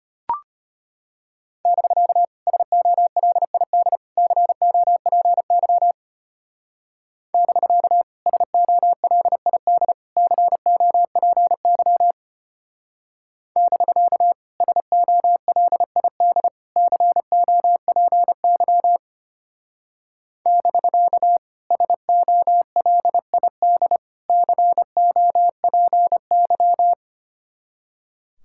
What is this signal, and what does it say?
Morse code, heard as a keyed tone, about 30 wpm, 700 Hz. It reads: <BK> SOLID COPY <BK> SOLID COPY <BK> SOLID COPY <BK> SOLID COPY
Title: Speed-Racing Common QSO Phrases Encore - 25wpm